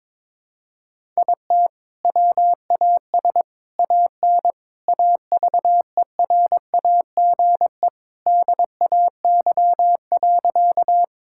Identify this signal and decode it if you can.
Morse code, heard as a keyed tone, about 22 wpm, 700 Hz. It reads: IT WAS AN AVERAGE DAY.
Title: Sentences from Top 1000 Words - Review Part 6 - 22wpm